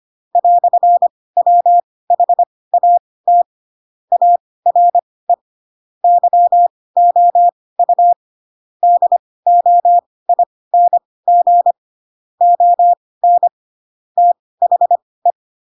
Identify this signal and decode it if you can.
Morse code, heard as a keyed tone, about 25 wpm, spaced as for 18 wpm, 700 Hz. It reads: "WHAT ARE YOU DOING ON THE
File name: Pinocchio / Pinocchio ch2 12 25 ik1hge